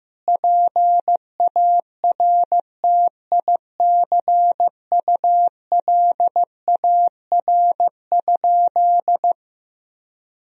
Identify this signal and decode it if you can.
Morse code, heard as a keyed tone, about 15 wpm, 700 Hz. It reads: PARTICULAR?
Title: Sentences from Top 1000 Words - Review Part 4 - 15wpm